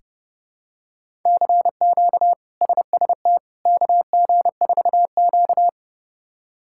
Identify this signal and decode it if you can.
Morse code, heard as a keyed tone, about 31 wpm, 700 Hz. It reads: CQ SST KG4Q
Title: K1USN SST QSO Practice - 30wpm